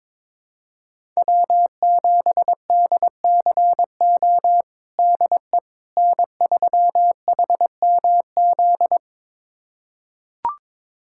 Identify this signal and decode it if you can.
Morse code, heard as a keyed tone, about 22 wpm, 700 Hz. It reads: W7DCO DE N3HMZ
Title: USA Callsign DE Callsign - 5 Characters - 22wpm